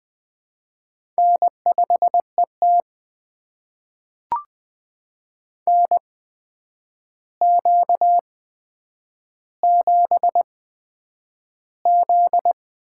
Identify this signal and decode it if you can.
Morse code, heard as a keyed tone, about 20 wpm, spaced as for 9 wpm, 700 Hz. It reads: N5ET NQ7Z
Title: USA Call Signs - 4 Characters - 8x Character Spacing - 20wpm